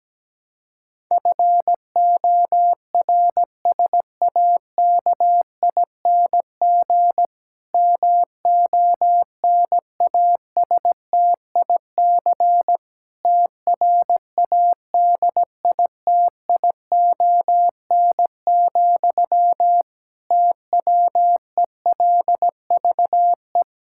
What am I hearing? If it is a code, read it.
Morse code, heard as a keyed tone, about 17 wpm, 700 Hz. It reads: FORSAKING MONASTIC TRADITION, TWELVE
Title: Pangram Challenge v2 - 17wpm